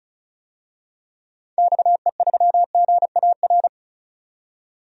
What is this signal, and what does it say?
Morse code, heard as a keyed tone, about 34 wpm, 700 Hz. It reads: XE3GAR
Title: DXCC-100 Call Signs - 6 Characters - 35wpm